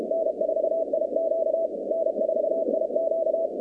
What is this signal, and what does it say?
Morse code, heard as a keyed tone, about 34 wpm, 650 Hz, with noise. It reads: N4IQ N4IQ
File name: 2019-cw-wpx-kt5j-k5pi